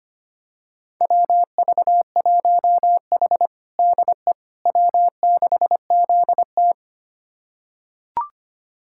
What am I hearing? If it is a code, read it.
Morse code, heard as a keyed tone, about 25 wpm, 700 Hz. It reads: WV1H DE W6ZT
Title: USA Callsign DE Callsign - 4 Characters - 25wpm